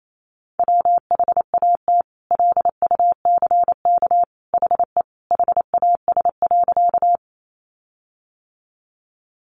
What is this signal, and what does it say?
Morse code, heard as a keyed tone, about 28 wpm, 700 Hz. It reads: WHAT LUCK HE HAS.
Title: Sentences from Top 1500 Words 28wpm